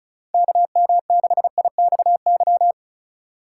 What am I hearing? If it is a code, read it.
Morse code, heard as a keyed tone, about 36 wpm, 700 Hz. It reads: KM6IXY
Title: USA Call Signs - 2x Character Spacing - 35wpm